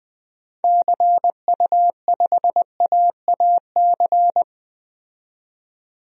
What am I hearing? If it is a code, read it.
Morse code, heard as a keyed tone, about 21 wpm, 700 Hz. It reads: CU5AAC
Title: DXCC-100 Call Signs - 6 Characters - 20wpm